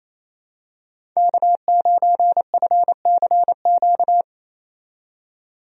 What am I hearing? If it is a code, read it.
Morse code, heard as a keyed tone, about 28 wpm, 700 Hz. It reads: K9FCQ
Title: USA Call Signs - 5 Characters - 28wpm